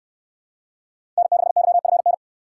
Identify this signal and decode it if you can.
Morse code, heard as a keyed tone, about 68 wpm, 700 Hz. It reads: N6YFN